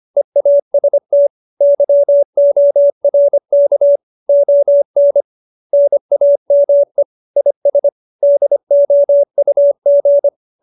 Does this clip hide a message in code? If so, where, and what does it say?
Yes: Morse code from 0.2 to 10.3 s, heard as a keyed tone, about 25 wpm, 570 Hz. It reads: EAST YORK ON NAME IS DOUG